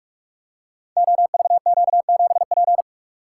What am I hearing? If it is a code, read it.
Morse code, heard as a keyed tone, about 46 wpm, 700 Hz. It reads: OVQ7P
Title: Code Groups - 5 Letter:Numbers - 45wpm